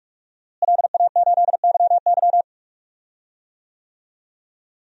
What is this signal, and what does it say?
Morse code, heard as a keyed tone, about 46 wpm, 700 Hz. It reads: LA8YY
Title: DXCC-100 Call Signs - 5 Characters - Repeated 3x - 45wpm